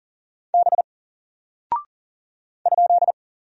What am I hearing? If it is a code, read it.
Morse code, heard as a keyed tone, about 39 wpm, 700 Hz. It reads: B ?
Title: Single Character (Letter:Number:Punctuation:Prosign) Alt 40wpm